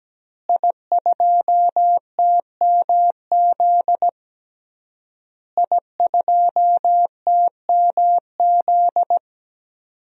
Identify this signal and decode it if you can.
Morse code, heard as a keyed tone, about 17 wpm, 700 Hz. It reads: I2TMZ I2TMZ
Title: DXCC-100 Call Signs - 5 Characters - Repeated 3x - 17wpm